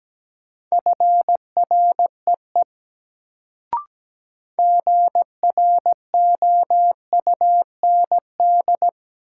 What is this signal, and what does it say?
Morse code, heard as a keyed tone, about 17 wpm, 700 Hz. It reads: FREE GROUND
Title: Sentences from Top 500 Words - New Word Drill - 17wpm